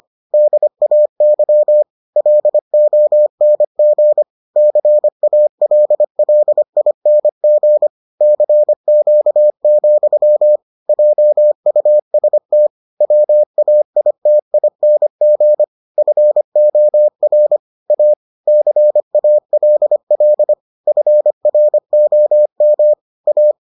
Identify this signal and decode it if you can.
Morse code, heard as a keyed tone, about 25 wpm, 600 Hz. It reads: DAY LONG CALLING CQ, JUST WAITING FOR A CALL FROM A